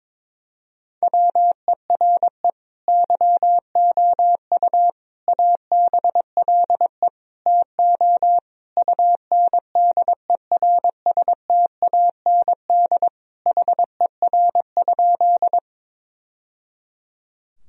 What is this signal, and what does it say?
Morse code, heard as a keyed tone, about 22 wpm, 700 Hz. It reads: WERE YOU ABLE TO UNDERSTAND HER?
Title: Sentences from Top 1000 Words - Review Part 2 - 22wpm